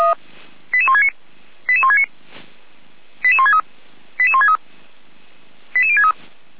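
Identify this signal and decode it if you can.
Five-tone selective call (ZVEI), heard as quick runs of five tones, tones of about 80 ms. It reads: 88169 88169 88163 88163 88063
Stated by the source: ZVEI five-tone selective call received on SDR, address 88363.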